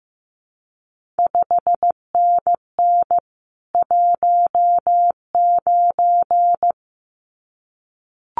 Morse code, heard as a keyed tone, about 15 wpm, 700 Hz. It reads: 5NN 19
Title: CQ WW DX Contest Exchange Encore - 15wpm